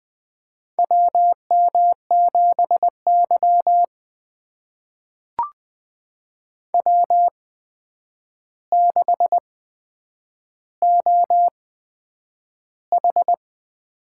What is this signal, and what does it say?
Morse code, heard as a keyed tone, about 20 wpm, spaced as for 9 wpm, 700 Hz. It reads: WM7Y W6OH
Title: USA Call Signs - 4 Characters - 8x Character Spacing - 20wpm